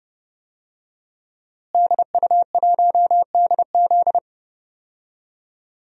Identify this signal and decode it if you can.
Morse code, heard as a keyed tone, about 32 wpm, 700 Hz. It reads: DU1DZ
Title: DXCC-100 Call Signs - 5 Characters - Repeated 3x - 30wpm